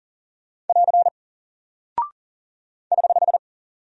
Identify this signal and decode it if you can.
Morse code, heard as a keyed tone, about 40 wpm, 700 Hz. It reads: <AR> <HH>
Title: Single Character (Letter:Number:Punctuation:Prosign) Alt 40wpm